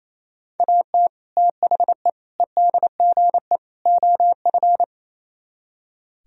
Morse code, heard as a keyed tone, about 37 wpm, 700 Hz. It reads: AT THE EDGE OF